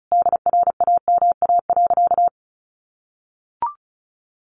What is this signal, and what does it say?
Morse code, heard as a keyed tone, about 34 wpm, 700 Hz. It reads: DRAMA.
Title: Sentences from Top 1500 Words 35wpm